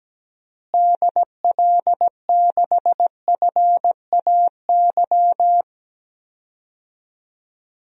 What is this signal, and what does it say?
Morse code, heard as a keyed tone, about 17 wpm, 700 Hz. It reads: DL6FAY
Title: DXCC-100 Call Signs - 6 Characters - Repeated 3x - 17wpm